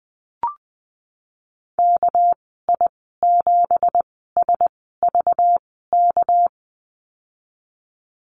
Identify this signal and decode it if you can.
Morse code, heard as a keyed tone, about 20 wpm, spaced as for 15 wpm, 700 Hz. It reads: KI7SVK
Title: 213 Lesson 34g - Number 8 - Call Signs 20wpm